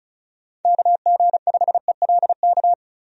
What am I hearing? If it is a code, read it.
Morse code, heard as a keyed tone, about 36 wpm, 700 Hz. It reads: KG5ELK
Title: USA Call Signs - 6 Characters - 35wpm